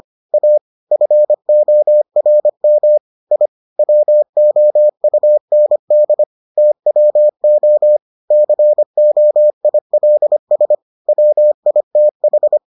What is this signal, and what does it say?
Morse code, heard as a keyed tone, about 25 wpm, 600 Hz. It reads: A FORM I WOUND TWO COILS WITH